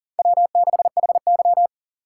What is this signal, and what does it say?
Morse code, heard as a keyed tone, about 40 wpm, 700 Hz. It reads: W6HY
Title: USA Call Signs - 4 Characters - 2x Character Spacing - 40wpm